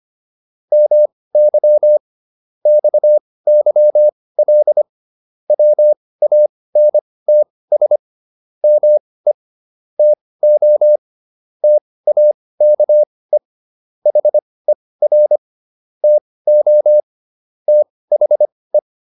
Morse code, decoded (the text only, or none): MY XYL WANTS ME TO TAKE HER TO THE